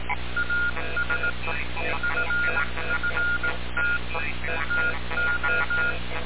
Morse code, heard as a keyed tone, about 18 wpm, 1420 Hz, with noise. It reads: AS FFT TO
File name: radiocap / REC z